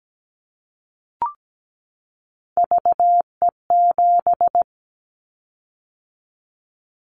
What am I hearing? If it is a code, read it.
Morse code, heard as a keyed tone, about 17 wpm, 700 Hz. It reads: VE7
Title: Callsign Prefixes for Top 100 DXCC - 17wpm